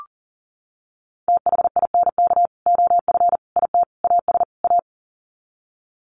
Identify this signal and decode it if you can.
Morse code, heard as a keyed tone, about 40 wpm, 700 Hz. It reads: THINK OF IT AS A
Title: Sets of 5 Words Encore2 40wpm